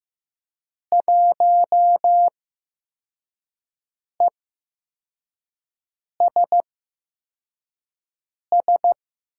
Morse code, heard as a keyed tone, about 15 wpm, spaced as for 4 wpm, 700 Hz. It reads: 1ESS
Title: USA Call Signs - 8x Character Spacing - 15wpm